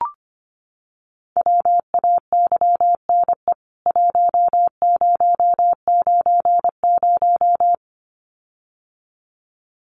Morse code, heard as a keyed tone, about 25 wpm, 700 Hz. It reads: WAYNE 1090
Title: CWT Exchange 25wpm